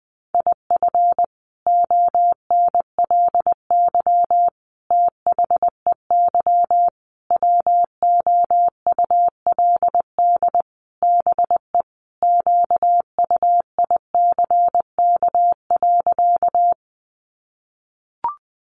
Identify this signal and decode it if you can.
Morse code, heard as a keyed tone, about 20 wpm, 700 Hz. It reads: IF ONLY THEY WOULD BE QUICK.
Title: Sentences from Top 500 Words 20wpm